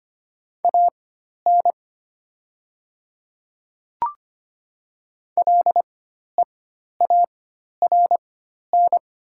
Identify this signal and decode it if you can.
Morse code, heard as a keyed tone, about 25 wpm, spaced as for 12 wpm, 700 Hz. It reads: AN LEARN
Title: 4x Character Spacing - Top 300 Words - 25wpm